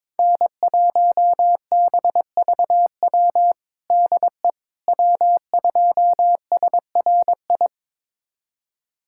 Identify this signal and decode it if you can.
Morse code, heard as a keyed tone, about 22 wpm, 700 Hz. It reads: N1BVW DE W2SRI
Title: USA Callsign DE Callsign - 5 Characters - 22wpm